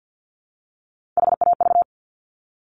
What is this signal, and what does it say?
Morse code, heard as a keyed tone, about 50 wpm, 690 Hz. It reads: HAV
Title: Top Three Letter Combinations in Words 50wpm